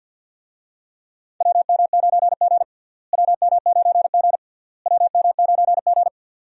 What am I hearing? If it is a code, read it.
Morse code, heard as a keyed tone, about 46 wpm, 700 Hz. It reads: WM9G WM9G WM9G
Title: USA Call Signs - 4 Characters - Repeated 3x - 50wpm